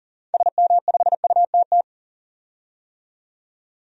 Morse code, heard as a keyed tone, about 40 wpm, 700 Hz. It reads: SM5UTT
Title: DXCC-100 Call Signs - 6 Characters - 40wpm